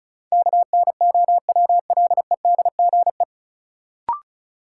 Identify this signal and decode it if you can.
Morse code, heard as a keyed tone, about 36 wpm, 700 Hz. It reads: KNOWLEDGE
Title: Sentences from Top 600 Words - New Word Drill - 35wpm